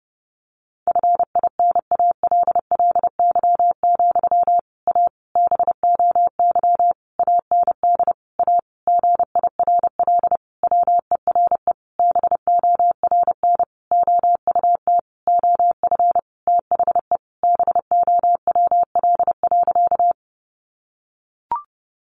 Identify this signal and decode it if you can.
Morse code, heard as a keyed tone, about 30 wpm, 700 Hz. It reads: FINALLY, A BOY AND A GIRL WERE BORN OUT OF THE BOWL.
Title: Sentences from Top 1200 Words 30wpm